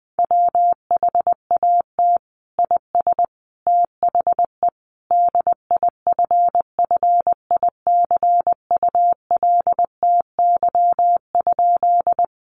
Morse code, heard as a keyed tone, about 20 wpm, 700 Hz. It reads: WHAT IS THE DIFFICULTY?
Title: Sentences from Top 1500 Words 20wpm